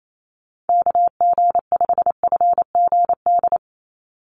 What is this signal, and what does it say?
Morse code, heard as a keyed tone, about 28 wpm, 700 Hz. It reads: KG5FGD
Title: USA Call Signs 28wpm